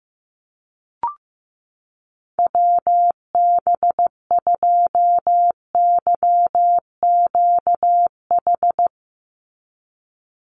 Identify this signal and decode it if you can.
Morse code, heard as a keyed tone, about 15 wpm, 700 Hz. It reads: WB2YQH
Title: Call Signs Repeated 2x - 15wpm